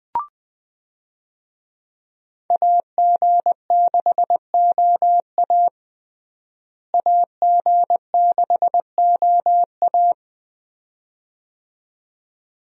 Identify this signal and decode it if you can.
Morse code, heard as a keyed tone, about 20 wpm, 700 Hz. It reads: AG6OA AG6OA
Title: USA Call Signs - 5 Characters - Repeated 2x - 20wpm